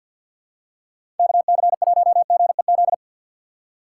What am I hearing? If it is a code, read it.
Morse code, heard as a keyed tone, about 53 wpm, 700 Hz. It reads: KC1GEZ
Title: USA Call Signs - 6x Character Spacing - 50wpm